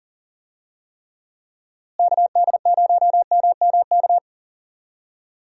Morse code, heard as a keyed tone, about 40 wpm, 700 Hz. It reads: KD0MMK
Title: USA Call Signs - 6 Characters - Repeated 3x - 40wpm